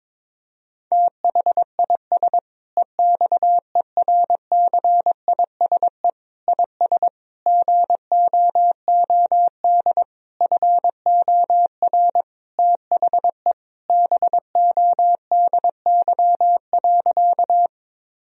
Morse code, heard as a keyed tone, about 22 wpm, 700 Hz. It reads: THIS EXERCISE IS GOOD FOR THE BODY.
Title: Sentences from Top 1000 Words - Review Part 4 - 22wpm